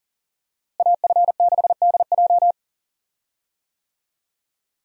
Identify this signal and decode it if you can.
Morse code, heard as a keyed tone, about 40 wpm, 700 Hz. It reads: AF6DJ
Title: USA Call Signs - 5 Characters - Repeated 3x - 40wpm